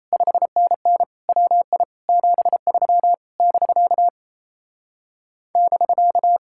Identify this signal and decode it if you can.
Morse code, heard as a keyed tone, about 37 wpm, 700 Hz. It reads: 5NN WI 73 <BK> <BK>